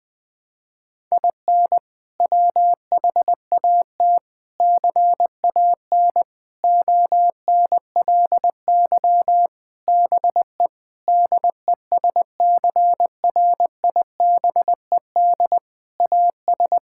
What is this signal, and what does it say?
Morse code, heard as a keyed tone, about 20 wpm, 700 Hz. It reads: IN WHAT CAN ONLY BE DESCRIBED AS
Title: Sets of 8 Words - 20wpm